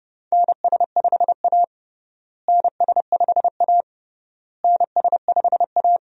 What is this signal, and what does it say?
Morse code, heard as a keyed tone, about 31 wpm, 700 Hz. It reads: NS5A NS5A NS5A
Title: USA Call Signs - 4 Characters - Repeated 3x - 30wpm